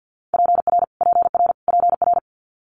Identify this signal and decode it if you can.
Morse code, heard as a keyed tone, about 50 wpm, 700 Hz. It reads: PR PR PR
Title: Multi-letter Phonemes Repeated 3x 50wpm